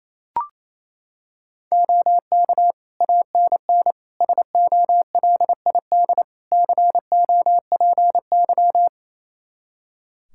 Morse code, heard as a keyed tone, about 28 wpm, 700 Hz. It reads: OK ANN SOLID COPY
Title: K1USN SST QSO Practice 28wpm